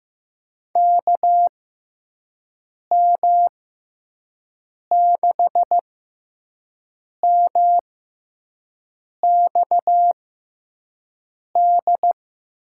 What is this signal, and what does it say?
Morse code, heard as a keyed tone, about 15 wpm, spaced as for 5 wpm, 700 Hz. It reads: KM6MXD
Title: USA Call Signs - 6 Characters - 6x Character Spacing - 15wpm